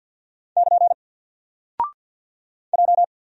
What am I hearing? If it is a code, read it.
Morse code, heard as a keyed tone, about 48 wpm, 700 Hz. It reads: <KN> J
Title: Single Character (Letter:Number:Punctuation:Prosign) 50wpm